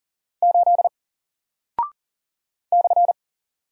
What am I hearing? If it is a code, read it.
Morse code, heard as a keyed tone, about 39 wpm, 700 Hz. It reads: 8 /
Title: Single Character (Letter:Number:Punctuation:Prosign) Alt 40wpm